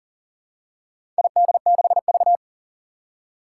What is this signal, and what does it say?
Morse code, heard as a keyed tone, about 40 wpm, 700 Hz. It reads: ID6V
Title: Four Challenging Letter-Number Combinations 40wpm